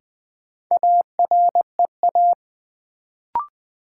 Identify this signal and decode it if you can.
Morse code, heard as a keyed tone, about 20 wpm, 700 Hz. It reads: AREA
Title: Sentences from Top 1000 Words - Review Part 2 - New Word Drill - 20wpm